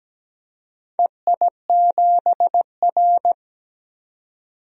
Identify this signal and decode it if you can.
Morse code, heard as a keyed tone, about 17 wpm, 700 Hz. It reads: EI7R
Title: DXCC-100 Call Signs - 4 Characters - 17wpm